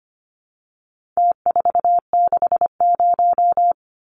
Morse code, heard as a keyed tone, about 25 wpm, 700 Hz. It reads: T460
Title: Numbers - Contest Style - 4-Digit 25wpm